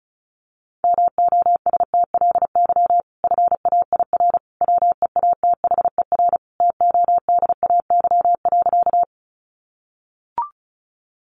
Morse code, heard as a keyed tone, about 34 wpm, 700 Hz. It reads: MOSTLY FAIR WEATHER TODAY.
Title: Sentences from Top 1300 Words 35wpm